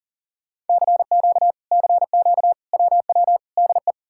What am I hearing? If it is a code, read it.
Morse code, heard as a keyed tone, about 52 wpm, 700 Hz. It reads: CQ CQ WW DE